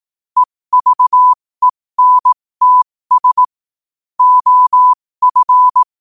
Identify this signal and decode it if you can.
Morse code, heard as a keyed tone, about 18 wpm, spaced as for 15 wpm, 1000 Hz. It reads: EVENTS OF
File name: qotdfiles / 15wpm